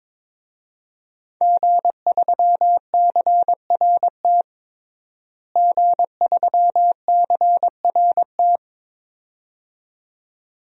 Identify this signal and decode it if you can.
Morse code, heard as a keyed tone, about 22 wpm, 700 Hz. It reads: G3CRT G3CRT
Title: DXCC-100 Call Signs - 5 Characters - Repeated 2x - 22wpm